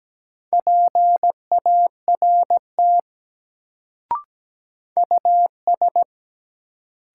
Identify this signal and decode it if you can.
Morse code, heard as a keyed tone, about 17 wpm, 700 Hz. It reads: PART US
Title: Sentences from Top 200 Words - New Word Drill - 17wpm